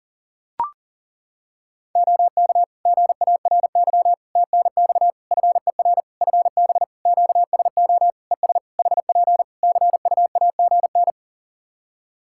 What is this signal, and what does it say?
Morse code, heard as a keyed tone, about 40 wpm, 700 Hz. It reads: OK GARY TNX FER FB QSO ES HP CUAGN
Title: Practice QSO - Exchange 4 - Ending - 40wpm